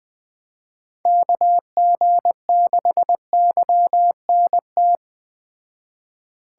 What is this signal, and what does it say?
Morse code, heard as a keyed tone, about 20 wpm, 700 Hz. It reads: KG6YNT